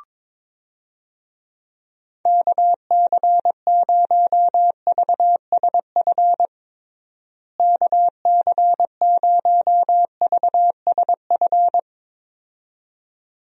Morse code, heard as a keyed tone, about 22 wpm, 700 Hz. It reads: KC0VSF KC0VSF
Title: USA Call Signs - 6 Characters - Repeated 2x - 22wpm